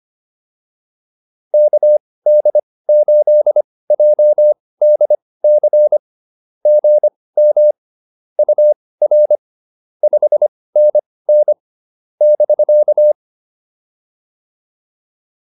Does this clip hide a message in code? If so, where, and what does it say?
Yes: Morse code from 1.5 to 13.1 s, heard as a keyed tone, about 25 wpm, spaced as for 18 wpm, 600 Hz. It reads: KD8JDC GM UR 5NN <BK>